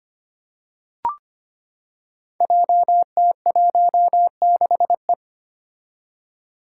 Morse code, heard as a keyed tone, about 25 wpm, 700 Hz. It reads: JT16E
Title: Five Challenging Letter-Number Combinations 25wpm